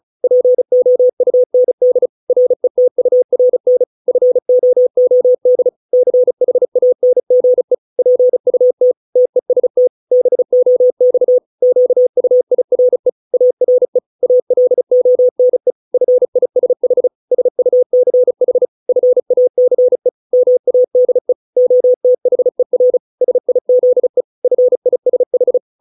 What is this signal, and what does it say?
Morse code, heard as a keyed tone, about 35 wpm, 500 Hz. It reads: POUND RETURN FOOD CHANGE PUT TEST BOX QUIRE ARE ALONE FISH SUCH FACE MADE OTHER SIZE FISH